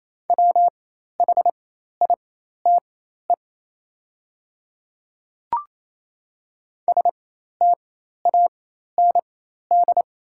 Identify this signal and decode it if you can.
Morse code, heard as a keyed tone, about 28 wpm, spaced as for 13 wpm, 700 Hz. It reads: WHITE STAND
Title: 4x Character Spacing - Top 300 Words - 28wpm